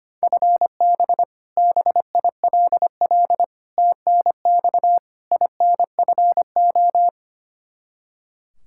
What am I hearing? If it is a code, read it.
Morse code, heard as a keyed tone, about 25 wpm, 700 Hz. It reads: FB BILL TNX INFO
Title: Practice QSO - Exchange 3 - More Info - 25wpm